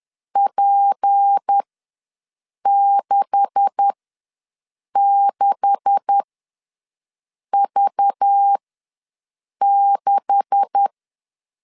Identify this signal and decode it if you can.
Morse code, heard as a keyed tone, about 10 wpm, spaced as for 6 wpm, 800 Hz. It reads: P66V6
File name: LEZIONI / 9 286 ws 10wpm 4032